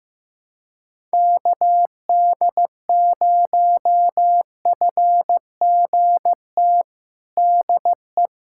Morse code, heard as a keyed tone, about 15 wpm, 700 Hz. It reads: KD0FGT DE
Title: USA Callsign DE Callsign - 6 Characters - 15wpm